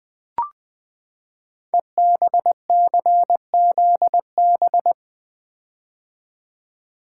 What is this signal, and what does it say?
Morse code, heard as a keyed tone, about 20 wpm, 700 Hz. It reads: EBCZB
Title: Code Groups - 5 Letters - 20wpm